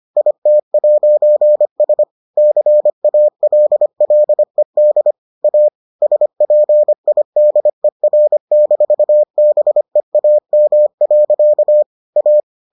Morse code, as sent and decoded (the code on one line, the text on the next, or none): .. - .----. ... / -.-. .- .-.. .-.. -.. / .- / ... .--. .. -.. .-. -....- -... .- -- .-.-.- / .-
IT'S CALLED A SPIDER-BEAM. A